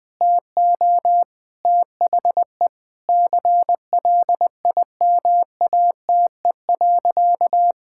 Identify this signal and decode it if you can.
Morse code, heard as a keyed tone, about 20 wpm, 700 Hz. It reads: TO THE CLIMATE.
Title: Sentences from Top 1600 Words 20wpm